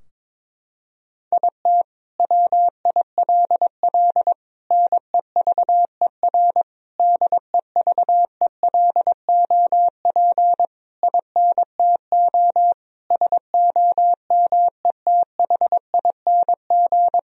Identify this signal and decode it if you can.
Morse code, heard as a keyed tone, about 22 wpm, 700 Hz. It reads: IT WILL NEVER DEVELOP INTO SOMETHING
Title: Sentences from Top 1000 Words - Review Part 2 - 22wpm